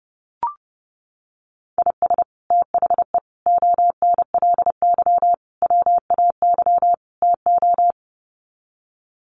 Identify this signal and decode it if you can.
Morse code, heard as a keyed tone, about 30 wpm, 700 Hz. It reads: IS THE ONLY WAY TO
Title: Sets of 5 Words Encore 30wpm